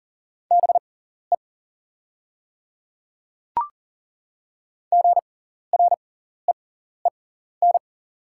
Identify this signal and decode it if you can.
Morse code, heard as a keyed tone, about 40 wpm, spaced as for 14 wpm, 690 Hz. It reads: BE GREEN
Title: 6x Character Spacing - Top 500 Words - 40wpm